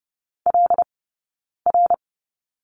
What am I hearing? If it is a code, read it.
Morse code, heard as a keyed tone, about 30 wpm, spaced as for 9 wpm, 690 Hz. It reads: LR
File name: course / 30:10wpm / 086 Lesson 15d - Number 5 - Two Characters 30wpm